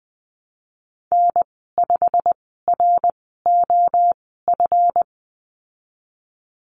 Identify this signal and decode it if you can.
Morse code, heard as a keyed tone, about 20 wpm, spaced as for 15 wpm, 700 Hz. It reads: N5ROF
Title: 161 Lesson 26g - Letter g - Call Signs 20wpm